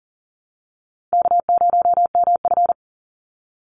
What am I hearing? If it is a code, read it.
Morse code, heard as a keyed tone, about 40 wpm, 700 Hz. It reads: K0MF
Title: USA Call Signs 40wpm